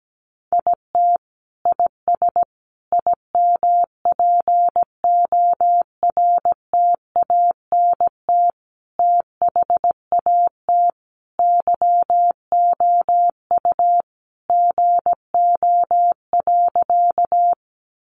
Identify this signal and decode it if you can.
Morse code, heard as a keyed tone, about 17 wpm, 700 Hz. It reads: IT IS IMPORTANT THAT YOU GO.
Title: Sentences from Top 600 Words 17wpm